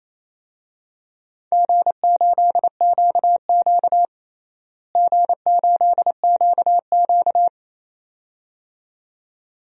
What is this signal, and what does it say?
Morse code, heard as a keyed tone, about 27 wpm, 700 Hz. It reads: G8QQ G8QQ
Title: DXCC-100 Call Signs - 4 Characters - Repeated 2x - 28wpm